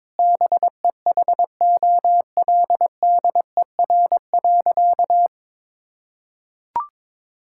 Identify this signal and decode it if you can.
Morse code, heard as a keyed tone, about 22 wpm, 700 Hz. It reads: BEHOLDER.
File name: practice / 22wpm / Sentences with 8-Word Phrases - 22wpm